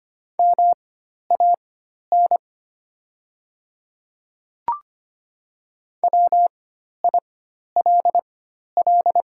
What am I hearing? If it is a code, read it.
Morse code, heard as a keyed tone, about 25 wpm, spaced as for 12 wpm, 700 Hz. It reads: MAN WILL
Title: 4x Character Spacing - Top 300 Words - 25wpm